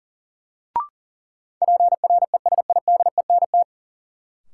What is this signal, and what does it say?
Morse code, heard as a keyed tone, about 40 wpm, 700 Hz. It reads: PRESIDENT
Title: Sentences from Top 1000 Words - New Word Drill - 40wpm